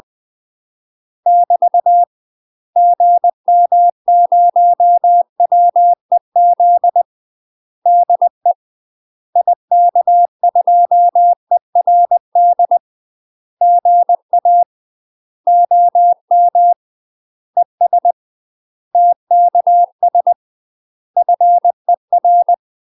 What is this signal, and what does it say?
Morse code, heard as a keyed tone, about 20 wpm, 700 Hz. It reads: <BT> GM0WEZ DE IK2ERD GA OM ES TKS FER
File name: UK MASTER QSOs TRAINING / 20WPM / UK MASTER QSOs 0068